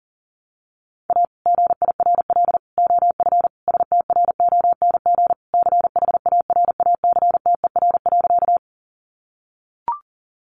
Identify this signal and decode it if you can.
Morse code, heard as a keyed tone, about 40 wpm, 700 Hz. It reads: A GIRL OF STRONG CHARACTER.
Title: Sentences from Top 1200 Words 40wpm